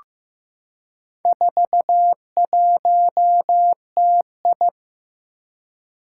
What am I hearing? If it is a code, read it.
Morse code, heard as a keyed tone, about 15 wpm, 700 Hz. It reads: 41TI
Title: Four Challenging Letter-Number Combinations 15wpm